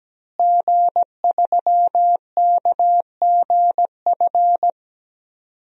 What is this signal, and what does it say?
Morse code, heard as a keyed tone, about 17 wpm, 700 Hz. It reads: G3KGF
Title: DXCC-100 Call Signs - 5 Characters - Repeated 2x - 17wpm